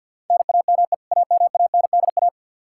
Morse code, heard as a keyed tone, about 52 wpm, 700 Hz. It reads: NAME AMANDA
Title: Practice QSO - Exchange 1 - Beginning - 50wpm